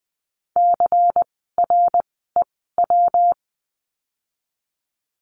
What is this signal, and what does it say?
Morse code, heard as a keyed tone, about 20 wpm, spaced as for 15 wpm, 700 Hz. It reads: CREW
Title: 115 Lesson 19f - Letter w - Words 20wpm